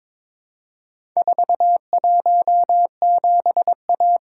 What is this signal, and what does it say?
Morse code, heard as a keyed tone, about 22 wpm, 700 Hz. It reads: 417A
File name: practice / 22wpm / Four Letter-Number Combinations - 22wpm